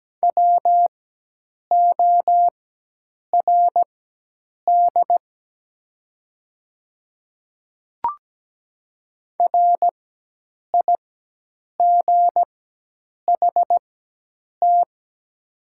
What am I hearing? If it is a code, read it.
Morse code, heard as a keyed tone, about 17 wpm, spaced as for 8 wpm, 690 Hz. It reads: WORD RIGHT
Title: 4x Character Spacing - Top 500 Words - 17wpm